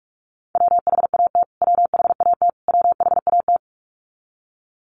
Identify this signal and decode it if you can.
Morse code, heard as a keyed tone, about 44 wpm, 700 Hz. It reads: WHAT WHAT WHAT
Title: Top 500 Words Repeated 45wpm